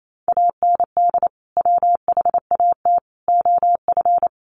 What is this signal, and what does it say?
Morse code, heard as a keyed tone, about 28 wpm, 700 Hz. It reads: AND WHAT OF
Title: Sentences from Top 1400 Words 28wpm